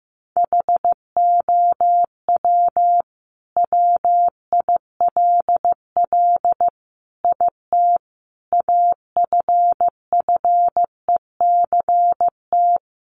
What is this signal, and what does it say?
Morse code, heard as a keyed tone, about 15 wpm, 700 Hz. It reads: HOW WILL IT AFFECT
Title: Sentences from Top 900 Words 15wpm